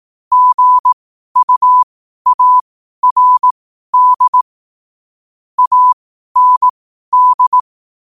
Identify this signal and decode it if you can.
Morse code, heard as a keyed tone, about 18 wpm, spaced as for 12 wpm, 1000 Hz. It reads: GUARD AND
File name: qotdfiles / 12wpm